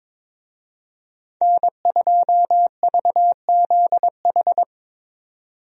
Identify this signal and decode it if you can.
Morse code, heard as a keyed tone, about 22 wpm, 700 Hz. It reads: N2VZH
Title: USA Call Signs - 5 Characters - 22wpm